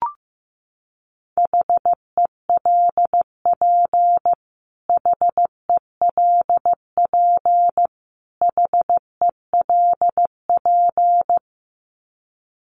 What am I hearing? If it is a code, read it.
Morse code, heard as a keyed tone, about 15 wpm, 700 Hz. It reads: HELP HELP HELP
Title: Top 300 Words Repeated 15wpm